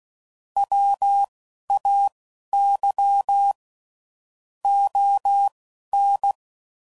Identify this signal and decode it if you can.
Morse code, heard as a keyed tone, about 16 wpm, spaced as for 11 wpm, 800 Hz. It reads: WAY ON
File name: CW Code Groups @10WPM / QSOs / QSO WA7QOX